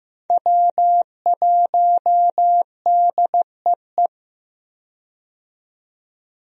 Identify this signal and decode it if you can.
Morse code, heard as a keyed tone, about 15 wpm, 700 Hz. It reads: W1DEE
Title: USA Call Signs - 5 Characters - 15wpm